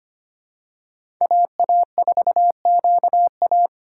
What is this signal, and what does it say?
Morse code, heard as a keyed tone, about 25 wpm, 700 Hz. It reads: AA4QA
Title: USA Call Signs - 5 Characters - 2x Character Spacing - 25wpm